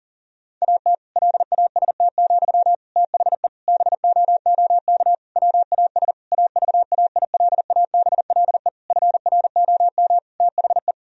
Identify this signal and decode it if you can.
Morse code, heard as a keyed tone, about 40 wpm, 700 Hz. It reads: AT LAST, THE BOOK WAS AVAILABLE FROM THE
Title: Sentences from Top 1000 Words - Review Part 3 - 40wpm